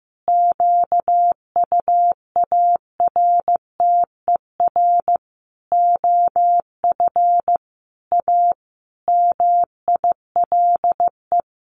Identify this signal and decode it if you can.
Morse code, heard as a keyed tone, about 15 wpm, 700 Hz. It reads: QUARTER OF A MILE
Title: Sentences from Top 1300 Words 15wpm